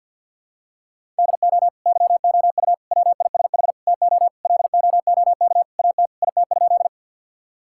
Morse code, heard as a keyed tone, about 50 wpm, 700 Hz. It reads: DO YOU WISH TO LOOK AT IT?
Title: Sentences from Top 1600 Words 50wpm